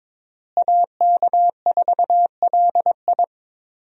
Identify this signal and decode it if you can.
Morse code, heard as a keyed tone, about 22 wpm, 700 Hz. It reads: AK4LI
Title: USA Call Signs - 5 Characters - Repeated 3x - 22wpm